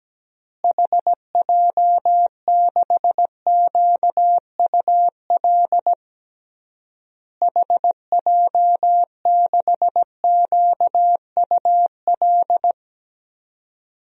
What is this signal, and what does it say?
Morse code, heard as a keyed tone, about 17 wpm, 700 Hz. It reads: HJ6QUL HJ6QUL
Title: DXCC-100 Call Signs - 6 Characters - Repeated 3x - 17wpm